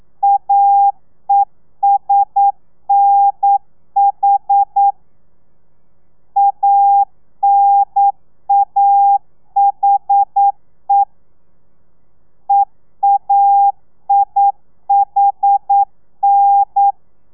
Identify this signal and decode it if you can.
Morse code, heard as a keyed tone, about 9 wpm, 800 Hz. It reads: AESNH ANAHE EAIHN
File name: telegrafia / i2tfj / Lez2 40